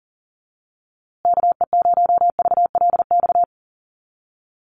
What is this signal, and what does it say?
Morse code, heard as a keyed tone, about 40 wpm, 700 Hz. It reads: KE0VLX
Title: USA Call Signs Repeated 3x 40wpm